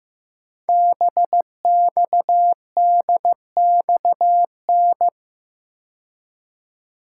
Morse code, heard as a keyed tone, about 15 wpm, 700 Hz. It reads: BXDXN
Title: Code Groups - 5 Letter:Numbers - 15wpm